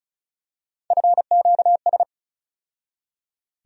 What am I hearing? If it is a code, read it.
Morse code, heard as a keyed tone, about 34 wpm, 700 Hz. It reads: FQS
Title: USA Call Sign Postfixes - 3 Letters 35wpm